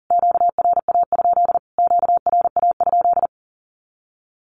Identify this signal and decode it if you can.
Morse code, heard as a keyed tone, about 40 wpm, 700 Hz. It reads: QRA? QRA?
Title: Q-Codes Repeated 3x 40wpm